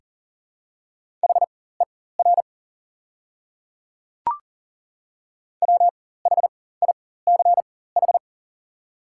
Morse code, heard as a keyed tone, about 40 wpm, spaced as for 19 wpm, 700 Hz. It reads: HER WHICH
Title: 4x Character Spacing - Top 100 Words - 40wpm